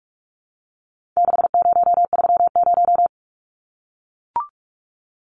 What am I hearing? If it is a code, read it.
Morse code, heard as a keyed tone, about 44 wpm, 700 Hz. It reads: 6030
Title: Numbers - 4-Digit 45wpm